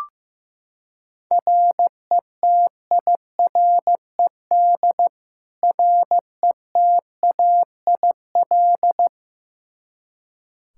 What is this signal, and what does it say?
Morse code, heard as a keyed tone, about 15 wpm, 700 Hz. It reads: RETIRED RETAIL
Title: K1USN SST QSO Practice 15wpm